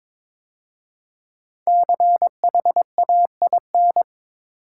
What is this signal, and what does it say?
Morse code, heard as a keyed tone, about 22 wpm, 700 Hz. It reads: CHAIN
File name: 5 Letter Words - First 3 Letters Given - 22wpm